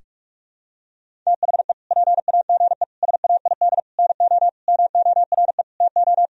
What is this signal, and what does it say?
Morse code, heard as a keyed tone, about 46 wpm, 700 Hz. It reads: THE PAGE SAID NO MORE TO
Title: Sentences from Top 1000 Words - Review Part 1 - 45wpm